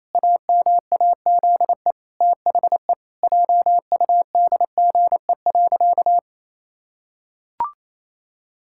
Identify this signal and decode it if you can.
Morse code, heard as a keyed tone, about 28 wpm, 700 Hz. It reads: AMAZE THE JUDGE.
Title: Pangram Challenge v2 - 28wpm